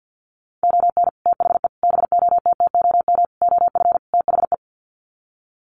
Morse code, heard as a keyed tone, about 50 wpm, 700 Hz. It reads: ON THE BOTTOM OF THE
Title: Sets of 5 Words Encore2 50wpm